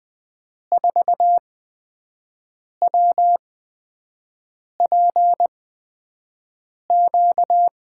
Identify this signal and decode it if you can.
Morse code, heard as a keyed tone, about 20 wpm, 700 Hz. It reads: 4 W P Q
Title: USA Call Signs - 5 Characters - 8x Character Spacing - 20wpm